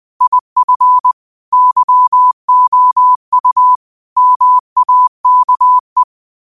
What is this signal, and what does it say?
Morse code, heard as a keyed tone, about 20 wpm, 1000 Hz. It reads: IF YOU MAKE
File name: qotdfiles / 20wpm